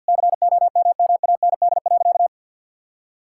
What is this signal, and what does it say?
Morse code, heard as a keyed tone, about 50 wpm, 700 Hz. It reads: COMMAND.
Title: Sentences from Top 1600 Words 50wpm